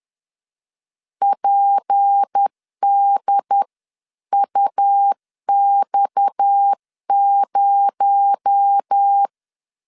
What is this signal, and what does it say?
Morse code, heard as a keyed tone, about 10 wpm, 800 Hz. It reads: PDUX0
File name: CW / DET / 1 6 1500C S C 10wpm 4252